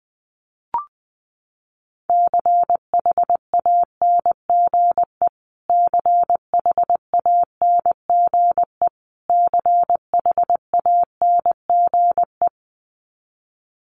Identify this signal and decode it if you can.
Morse code, heard as a keyed tone, about 20 wpm, 700 Hz. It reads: CHANGE CHANGE CHANGE
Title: Top 300 Words Repeated 20wpm